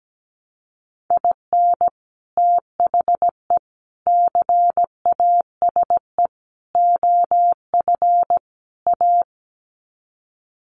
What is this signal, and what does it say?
Morse code, heard as a keyed tone, about 17 wpm, 700 Hz. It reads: IN THE CASE OF A
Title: Sets of 5 Words Encore 17wpm